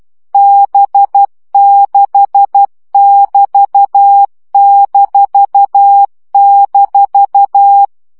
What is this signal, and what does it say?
Morse code, heard as a keyed tone, about 12 wpm, 800 Hz. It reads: B6<BT>--
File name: CW gyakorlas b 6 egyenlo minusz 10 baud